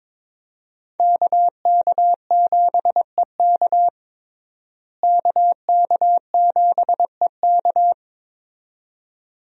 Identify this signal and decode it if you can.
Morse code, heard as a keyed tone, about 21 wpm, 700 Hz. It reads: KK7EK KK7EK
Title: USA Call Signs - 5 Characters - Repeated 2x - 22wpm